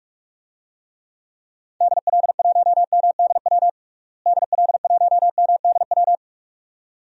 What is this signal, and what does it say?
Morse code, heard as a keyed tone, about 47 wpm, 700 Hz. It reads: DL1MDW DL1MDW
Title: DXCC-100 Call Signs - 6 Characters - Repeated 2x - 45wpm